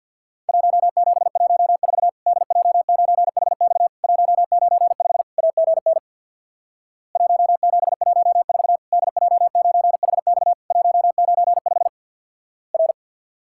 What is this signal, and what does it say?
Morse code, heard as a keyed tone, about 48 wpm, 700 Hz. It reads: 1714 DJ9HX 195 AGN 1714 DJ9HX 195 R
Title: Worked All Europe DX Contest - QTC Exchange with Repeats - 50wpm